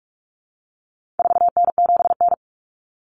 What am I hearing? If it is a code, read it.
Morse code, heard as a keyed tone, about 44 wpm, 700 Hz. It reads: 4N7N
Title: Common Cut Numbers - 4-Digit 45wpm